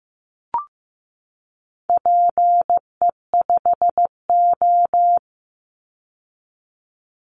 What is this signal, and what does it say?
Morse code, heard as a keyed tone, about 15 wpm, 700 Hz. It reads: PE5O
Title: Call Signs Repeated 3x - 15wpm